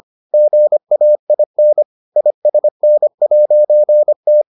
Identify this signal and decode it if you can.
Morse code, heard as a keyed tone, about 25 wpm, 600 Hz. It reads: GAIN ISN'T